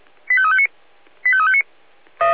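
Five-tone selective call (ZVEI), heard as quick runs of five tones, tones of about 70 ms. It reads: 85359 85359
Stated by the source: ZVEI five-tone selective call received by SDR.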